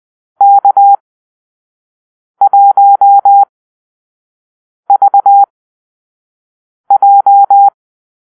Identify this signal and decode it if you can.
Morse code, heard as a keyed tone, about 20 wpm, spaced as for 5 wpm, 800 Hz. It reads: K1VJ
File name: arrl / Call-Sign Drill-4